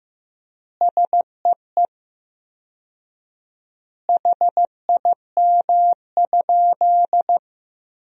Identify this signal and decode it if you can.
Morse code, heard as a keyed tone, about 16 wpm, 700 Hz. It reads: SEE HIM?
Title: 4x Word-Spacing - Sentences from Top 100 Words - 15wpm